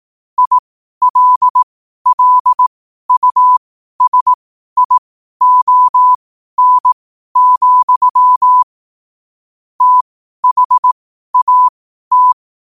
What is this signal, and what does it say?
Morse code, heard as a keyed tone, about 18 wpm, spaced as for 12 wpm, 1000 Hz. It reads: ILLUSION, THAT
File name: qotdfiles / 12wpm